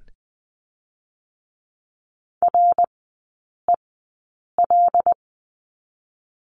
Morse code, heard as a keyed tone, about 19 wpm, 700 Hz. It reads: R E L
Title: 198 Lesson 32e - Letter k - Three Characters 20wpm